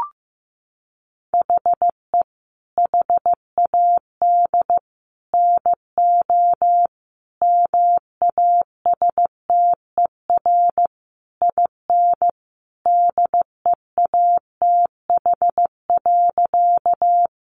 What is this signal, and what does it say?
Morse code, heard as a keyed tone, about 15 wpm, 700 Hz. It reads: HE HAD NO MASTER IN DEATH.
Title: Sentences from Top 1300 Words 15wpm